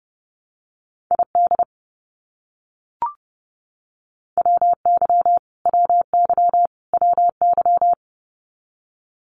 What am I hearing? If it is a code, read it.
Morse code, heard as a keyed tone, about 30 wpm, 700 Hz. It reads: ID WY WY WY
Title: US States Repeated 3x 30wpm